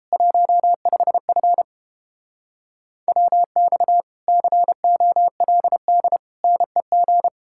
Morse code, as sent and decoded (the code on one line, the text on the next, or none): .---- ..... ..-. / .-- -..- / -.-. --- .-.. -.. / -. --.
15F WX COLD NEG